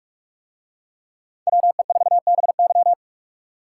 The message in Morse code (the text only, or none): WE4BY